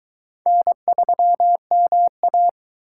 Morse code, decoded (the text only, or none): N3MA